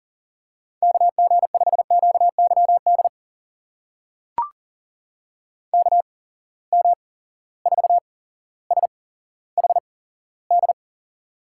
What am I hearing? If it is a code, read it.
Morse code, heard as a keyed tone, about 40 wpm, spaced as for 17 wpm, 700 Hz. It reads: KG5QYD KM4SHD
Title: USA Call Signs - 6 Characters - 8x Character Spacing - 40wpm